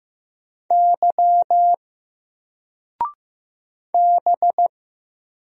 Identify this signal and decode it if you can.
Morse code, heard as a keyed tone, about 15 wpm, 700 Hz. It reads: Y B
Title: Single Letter-Number 15wpm